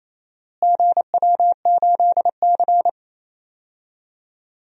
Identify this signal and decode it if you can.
Morse code, heard as a keyed tone, about 28 wpm, 700 Hz. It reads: GW8C
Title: DXCC-100 Call Signs - 4 Characters - Repeated 2x - 28wpm